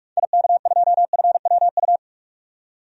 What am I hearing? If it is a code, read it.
Morse code, heard as a keyed tone, about 46 wpm, 700 Hz. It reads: IK2FWU
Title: DXCC-100 Call Signs - 6 Characters - 45wpm